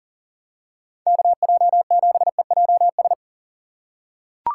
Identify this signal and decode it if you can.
Morse code, heard as a keyed tone, about 40 wpm, 700 Hz. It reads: KJ7EJS
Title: USA Call Signs - 6 Characters - 4x Character Spacing - 40wpm